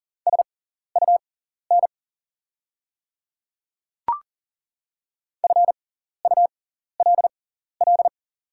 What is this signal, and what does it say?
Morse code, heard as a keyed tone, about 40 wpm, spaced as for 14 wpm, 700 Hz. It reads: SUN FULL
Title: 6x Character Spacing - Top 500 Words - 40wpm